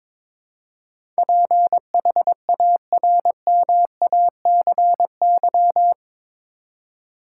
Morse code, heard as a keyed tone, about 22 wpm, 700 Hz. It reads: PHARMACY
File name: practice / 22wpm / Eight Letter Words - 22wpm